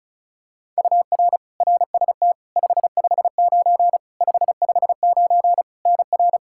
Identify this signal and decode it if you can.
Morse code, heard as a keyed tone, about 36 wpm, 700 Hz. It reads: UR RST 559 559 NR
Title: K1USN SST QSO Practice 35wpm